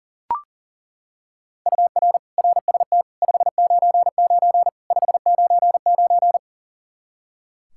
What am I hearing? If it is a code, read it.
Morse code, heard as a keyed tone, about 40 wpm, 700 Hz. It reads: UR RST 599 599
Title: Practice QSO - Exchange 1 - Beginning - 40wpm